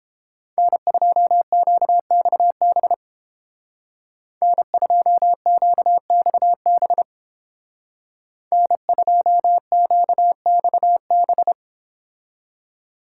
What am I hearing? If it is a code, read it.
Morse code, heard as a keyed tone, about 29 wpm, 700 Hz. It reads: N2QXB N2QXB N2QXB